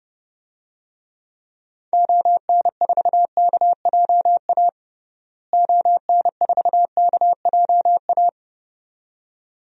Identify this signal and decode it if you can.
Morse code, heard as a keyed tone, about 32 wpm, 700 Hz. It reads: ON4KJA ON4KJA
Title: DXCC-100 Call Signs - 6 Characters - Repeated 2x - 30wpm